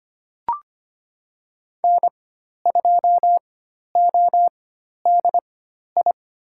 Morse code, heard as a keyed tone, about 25 wpm, spaced as for 12 wpm, 700 Hz. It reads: N2ODI
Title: USA Call Signs - 5 Characters - 4x Character Spacing - 25wpm